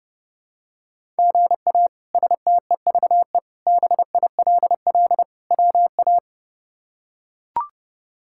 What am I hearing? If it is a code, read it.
Morse code, heard as a keyed tone, about 30 wpm, 700 Hz. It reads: GA STEVE BILL WA
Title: K1USN SST QSO Practice - 30wpm